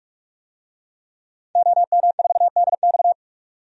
Morse code, heard as a keyed tone, about 46 wpm, 700 Hz. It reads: OM4DX
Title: DXCC-100 Call Signs - 5 Characters - 45wpm